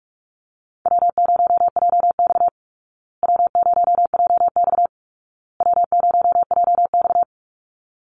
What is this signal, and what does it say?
Morse code, heard as a keyed tone, about 44 wpm, 700 Hz. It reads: W0JX W0JX W0JX
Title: Easy Call Signs Repeated 3x - 45wpm